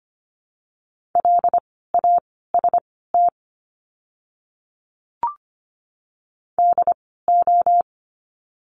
Morse code, heard as a keyed tone, about 25 wpm, spaced as for 16 wpm, 700 Hz. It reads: LAST DO
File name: course / 25:10wpm / 094 Lesson 16f - Letter c - Words 25wpm